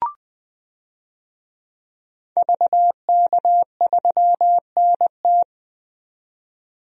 Morse code, heard as a keyed tone, about 20 wpm, 700 Hz. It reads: VK3NT
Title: DXCC-100 Call Signs - 5 Characters - 20wpm